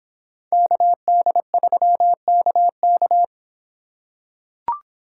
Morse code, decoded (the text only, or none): KD3KK